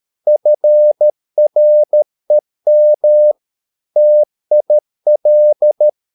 Morse code, heard as a keyed tone, about 13 wpm, 600 Hz. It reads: FREM TIL
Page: Ferie 13 wpm